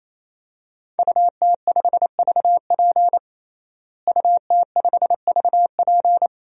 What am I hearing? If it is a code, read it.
Morse code, heard as a keyed tone, about 28 wpm, 700 Hz. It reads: UT5VP UT5VP
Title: DXCC-100 Call Signs - 5 Characters - Repeated 3x - 28wpm